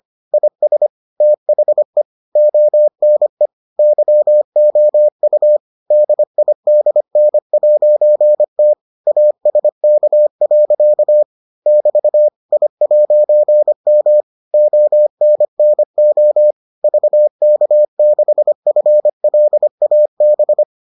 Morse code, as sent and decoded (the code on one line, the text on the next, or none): .. ... / - .... / --- -. / -.-- --- ..- / -.. .. -.. -. .----. - / .- ... -.- .-.-.- / -...- / .. .----. -- / --- -. -. --- / ...- -.- -.... ..-. .-.. .- -...
IS THE ONE YOU DIDN'T ASK. <BT> I'M ONNO VK6FLAB